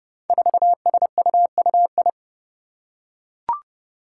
Morse code, heard as a keyed tone, about 32 wpm, 700 Hz. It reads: 4SUUI
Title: Five Challenging Letter-Number Combinations 30wpm